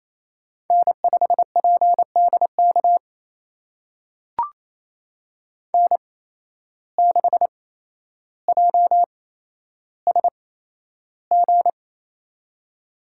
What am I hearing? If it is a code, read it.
Morse code, heard as a keyed tone, about 28 wpm, spaced as for 12 wpm, 700 Hz. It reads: N5PDK N6JSG
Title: USA Call Signs - 8x Character Spacing - 28wpm